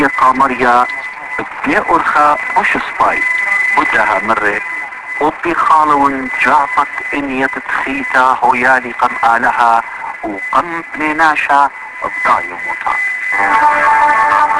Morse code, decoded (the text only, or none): none